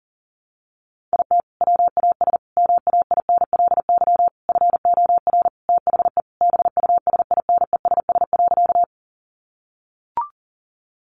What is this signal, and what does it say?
Morse code, heard as a keyed tone, about 40 wpm, 700 Hz. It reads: IT WAS MAINLY FOR THE BUSINESS.
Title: Sentences from Top 1400 Words 40wpm